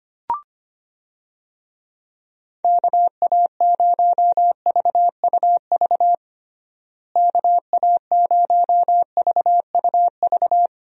Morse code, heard as a keyed tone, about 25 wpm, 700 Hz. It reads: KA0VUV KA0VUV
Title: USA Call Signs - 6 Characters - Repeated 2x - 25wpm